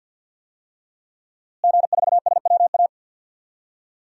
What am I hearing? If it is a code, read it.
Morse code, heard as a keyed tone, about 51 wpm, 700 Hz. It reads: G4SWA
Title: DXCC-100 Call Signs - 5 Characters - Repeated 3x - 50wpm